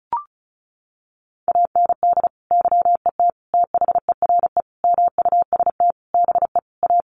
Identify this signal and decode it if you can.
Morse code, heard as a keyed tone, about 34 wpm, 700 Hz. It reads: AND YET THERE MUST BE A
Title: Sentences from Top 1300 Words 35wpm